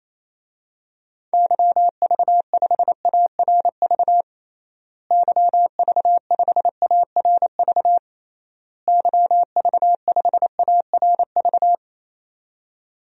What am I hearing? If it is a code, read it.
Morse code, heard as a keyed tone, about 28 wpm, 700 Hz. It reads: YV5ARV YV5ARV YV5ARV
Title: DXCC-100 Call Signs - 6 Characters - Repeated 3x - 28wpm